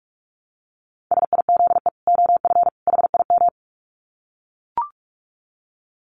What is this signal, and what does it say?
Morse code, heard as a keyed tone, about 44 wpm, 700 Hz. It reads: SIZE OF HIM
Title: Dit Challenge 45wpm